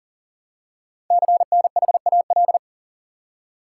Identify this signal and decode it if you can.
Morse code, heard as a keyed tone, about 39 wpm, 700 Hz. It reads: CNHAL
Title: Code Groups - 5 Letters - 40wpm